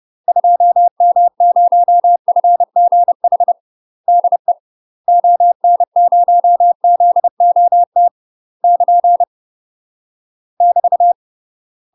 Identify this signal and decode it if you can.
Morse code, heard as a keyed tone, about 30 wpm, 700 Hz. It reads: 2M0FGH DE ON0ZOT <KN> <BT>
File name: UK MASTER QSOs TRAINING / 30WPM / UK MASTER QSOs 0002